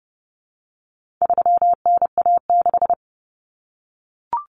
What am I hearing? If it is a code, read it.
Morse code, heard as a keyed tone, about 30 wpm, 700 Hz. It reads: 3NA6
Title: Common Cut Numbers - 4-Digit 30wpm